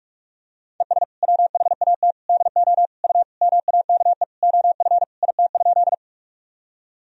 Morse code, heard as a keyed tone, about 43 wpm, 700 Hz. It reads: ES WHAT DO U MAKE OF IT?
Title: Sentences from Top 100 Words and Abbreviations 45wpm